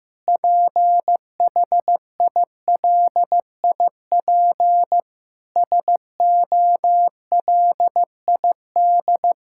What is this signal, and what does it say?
Morse code, heard as a keyed tone, about 16 wpm, 700 Hz. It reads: PHILIP SOLID
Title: K1USN SST QSO Practice 15wpm